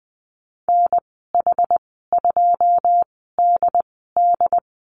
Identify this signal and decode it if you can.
Morse code, heard as a keyed tone, about 19 wpm, spaced as for 14 wpm, 700 Hz. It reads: NH2DD
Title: 161 Lesson 26g - Letter g - Call Signs 20wpm